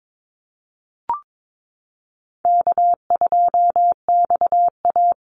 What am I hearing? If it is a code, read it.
Morse code, heard as a keyed tone, about 22 wpm, 700 Hz. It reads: K2XA
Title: Easy Call Signs Repeated 3x - 22wpm